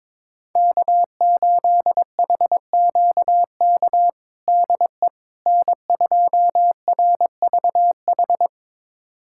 Morse code, heard as a keyed tone, about 22 wpm, 700 Hz. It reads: K8HQK DE N2RVH
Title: USA Callsign DE Callsign - 5 Characters - 22wpm